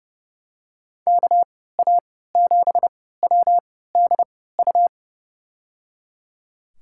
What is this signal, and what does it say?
Morse code, heard as a keyed tone, about 30 wpm, spaced as for 17 wpm, 700 Hz. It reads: KA7WDU
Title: USA Call Signs - 6 Characters - 3x Character Spacing - 30wpm